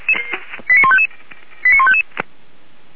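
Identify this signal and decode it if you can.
Five-tone selective call (ZVEI), heard as quick runs of five tones, tones of about 70 ms. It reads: 89155 89155
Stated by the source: ZVEI five-tone call received on SDR, from the Feuerwehr Orlamünde fire brigade.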